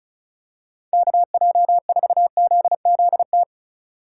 Morse code, heard as a keyed tone, about 34 wpm, 700 Hz. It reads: KJ4ZZT
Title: USA Call Signs - 6 Characters - 3x Character Spacing - 35wpm